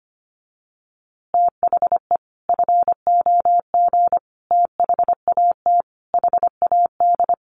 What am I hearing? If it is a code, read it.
Morse code, heard as a keyed tone, about 25 wpm, 700 Hz. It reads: THE FOG THAT HAD
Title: A Sound Of Thunder 25wpm